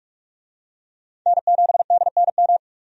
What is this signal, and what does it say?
Morse code, heard as a keyed tone, about 42 wpm, 700 Hz. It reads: N7DNM
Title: USA Call Signs - 5 Characters - 45wpm